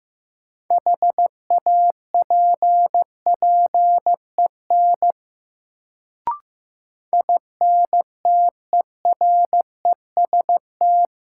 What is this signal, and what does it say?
Morse code, heard as a keyed tone, about 15 wpm, 700 Hz. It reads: HAPPEN INTEREST
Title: Sets of 5 Words Encore - New Word Drill - 15wpm